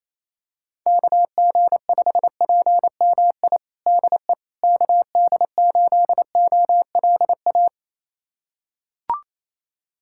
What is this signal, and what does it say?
Morse code, heard as a keyed tone, about 28 wpm, 700 Hz. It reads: KG5PMI DE KD8OLA
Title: USA Callsign DE Callsign - 6 Characters - 28wpm